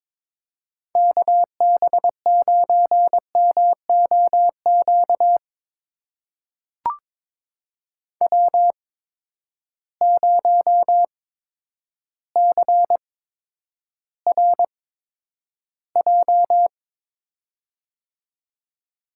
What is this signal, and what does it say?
Morse code, heard as a keyed tone, about 22 wpm, spaced as for 10 wpm, 700 Hz. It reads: KB9MOQ W0CRJ
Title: USA Call Signs - 8x Character Spacing - 22wpm